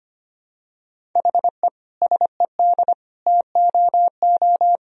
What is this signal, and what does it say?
Morse code, heard as a keyed tone, about 25 wpm, 700 Hz. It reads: HE SED TOO
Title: Sentences from Top 200 Words and Abbreviations 25wpm